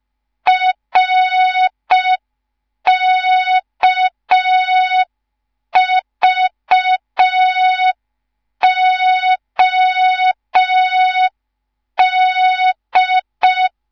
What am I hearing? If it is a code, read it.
Morse code, heard as a keyed tone, about 5 wpm, 3700 Hz. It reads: RKVOD